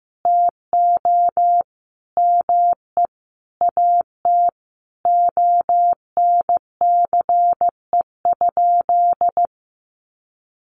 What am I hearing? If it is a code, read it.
Morse code, heard as a keyed tone, about 15 wpm, 700 Hz. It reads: TO ME AT ONCE?
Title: Sentences from Top 1100 Words 15wpm